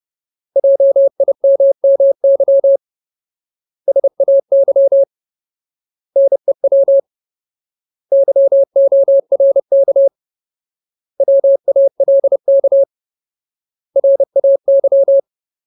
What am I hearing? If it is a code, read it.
Morse code, heard as a keyed tone, about 30 wpm, 550 Hz. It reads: JIMMY SAY NEW YORK WALK RAY
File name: Ultimate Word List / 30WPM Ultimate